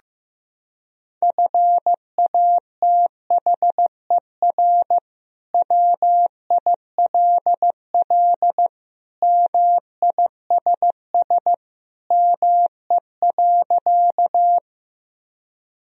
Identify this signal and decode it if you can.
Morse code, heard as a keyed tone, about 15 wpm, 700 Hz. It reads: FATHER WILL MISS ME.
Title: Sentences from Top 1000 Words - Review Part 3 - 15wpm